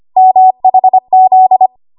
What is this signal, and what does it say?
Morse code, heard as a keyed tone, about 25 wpm, 750 Hz. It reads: MHZ
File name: Morse / 25WPM